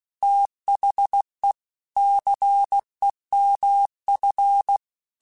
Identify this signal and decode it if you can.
Morse code, heard as a keyed tone, about 16 wpm, 800 Hz. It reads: THE CEMF
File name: Technical Subjects / Technical Subjects-mp3s / XL @15